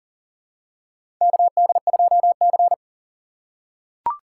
Morse code, heard as a keyed tone, about 40 wpm, 690 Hz. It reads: KD2C
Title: USA Call Signs - 4x Character Spacing - 40wpm